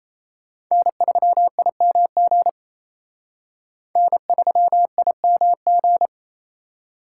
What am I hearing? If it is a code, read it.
Morse code, heard as a keyed tone, about 36 wpm, 700 Hz. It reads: N3IMG N3IMG